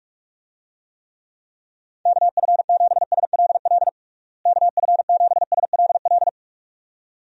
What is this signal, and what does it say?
Morse code, heard as a keyed tone, about 41 wpm, 700 Hz. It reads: KF7SLL KF7SLL
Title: USA Call Signs - 6 Characters - Repeated 2x - 45wpm